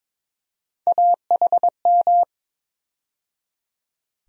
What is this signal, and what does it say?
Morse code, heard as a keyed tone, about 21 wpm, 700 Hz. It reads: AHM
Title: USA Call Sign Postfixes - 3 Letters 22wpm